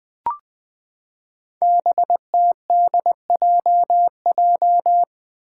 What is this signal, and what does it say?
Morse code, heard as a keyed tone, about 20 wpm, 700 Hz. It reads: BTDJJ
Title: Five Challenging Letter-Number Combinations 20wpm